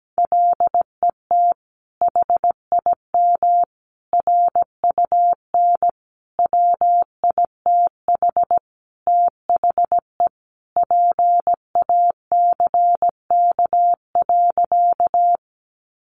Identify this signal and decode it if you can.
Morse code, heard as a keyed tone, about 17 wpm, 700 Hz. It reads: LET HIM RUN WITH THE PACK.
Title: Sentences from Top 1300 Words 17wpm